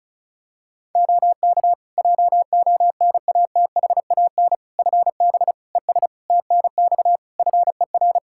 Morse code, heard as a keyed tone, about 36 wpm, 700 Hz. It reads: OK JONATHAN FB ES TNX FER
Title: Practice QSO - Exchange 2 - Info - 35wpm